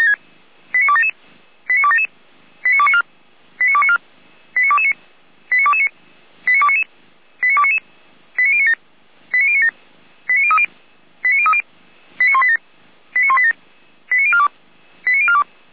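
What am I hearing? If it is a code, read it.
Five-tone selective call (ZVEI), heard as quick runs of five tones, tones of about 70 ms. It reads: 89277 89277 89294 89294 89229 89229 89299 89299 89097 89097 89030 89030 89187 89187 89042 89042
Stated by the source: ZVEI five-tone selective call 89310, received by SDR.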